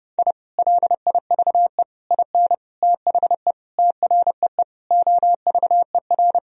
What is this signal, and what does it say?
Morse code, heard as a keyed tone, about 30 wpm, 700 Hz. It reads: I LIVE IN THE TREE OVER
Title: Sentences from Top 1000 Words - Review Part 1 - 30wpm